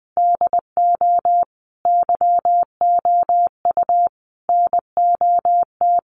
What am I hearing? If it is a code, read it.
Morse code, heard as a keyed tone, about 20 wpm, 700 Hz. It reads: DO YOU NOT
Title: Sentences from Top 1400 Words 20wpm